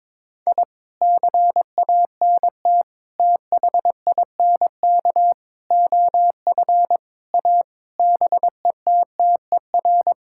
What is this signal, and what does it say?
Morse code, heard as a keyed tone, about 22 wpm, 700 Hz. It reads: I CANT THINK OF A BETTER
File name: practice / 22wpm / Sets of 8 Words - 22wpm